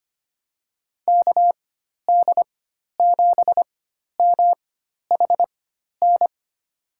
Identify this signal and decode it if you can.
Morse code, heard as a keyed tone, about 25 wpm, spaced as for 12 wpm, 700 Hz. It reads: KD7MHN
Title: USA Call Signs - 6 Characters - 4x Character Spacing - 25wpm